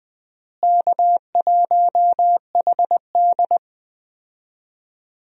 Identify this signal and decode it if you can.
Morse code, heard as a keyed tone, about 20 wpm, 700 Hz. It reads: K1HD
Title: USA Call Signs - 4 Characters - Repeated 3x - 20wpm